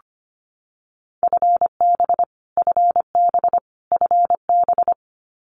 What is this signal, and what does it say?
Morse code, heard as a keyed tone, about 25 wpm, 700 Hz. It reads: FB FB FB
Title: Abbreviations Repeated 3x - 25wpm